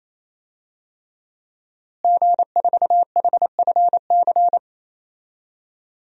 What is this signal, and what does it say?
Morse code, heard as a keyed tone, about 28 wpm, 700 Hz. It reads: G4HFC
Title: DXCC-100 Call Signs - 5 Characters - Repeated 3x - 28wpm